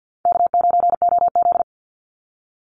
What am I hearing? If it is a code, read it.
Morse code, heard as a keyed tone, about 50 wpm, 700 Hz. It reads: K9OZ
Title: NA SOTA Chaser Callsigns 50wpm